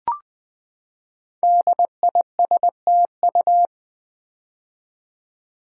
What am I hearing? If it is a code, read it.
Morse code, heard as a keyed tone, about 20 wpm, 700 Hz. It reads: DISTU
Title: Build A Word - 5 Letters At A Time - 10-Letter Words - 20wpm